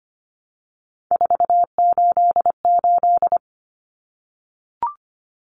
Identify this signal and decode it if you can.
Morse code, heard as a keyed tone, about 25 wpm, 700 Hz. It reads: 488
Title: Common Cut Numbers - 3-Digit 25wpm